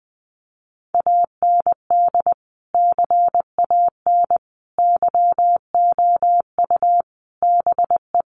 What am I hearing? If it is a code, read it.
Morse code, heard as a keyed tone, about 20 wpm, 700 Hz. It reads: AND CAN YOU BE
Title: Sentences from Top 1100 Words 20wpm